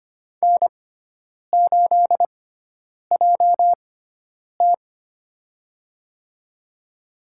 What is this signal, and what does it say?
Morse code, heard as a keyed tone, about 25 wpm, spaced as for 9 wpm, 700 Hz. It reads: N8JT
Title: USA Call Signs - 4 Characters - 6x Character Spacing - 25wpm